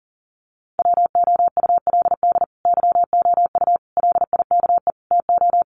Morse code, heard as a keyed tone, about 37 wpm, 700 Hz. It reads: WOULD YOU LIKE TO
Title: Sets of 4 Words 40wpm